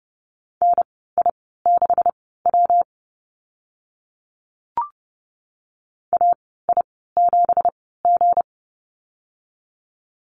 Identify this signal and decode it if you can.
Morse code, heard as a keyed tone, about 30 wpm, spaced as for 17 wpm, 700 Hz. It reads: NI6W AI7G
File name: Lesson 37g - Letter x - Call Signs 30wpm